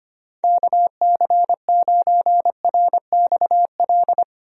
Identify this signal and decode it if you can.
Morse code, heard as a keyed tone, about 25 wpm, 700 Hz. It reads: KC9RXL
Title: USA Call Signs - 6 Characters - 25wpm